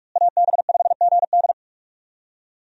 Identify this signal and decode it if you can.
Morse code, heard as a keyed tone, about 45 wpm, 700 Hz. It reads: AB5GD
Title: USA Call Signs - 5 Characters - Repeated 2x - 45wpm